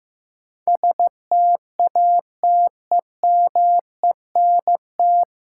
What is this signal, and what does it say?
Morse code, heard as a keyed tone, about 15 wpm, 700 Hz. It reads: STATEMENT
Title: Sentences from Top 1200 Words - New Word Drill Alt - 15wpm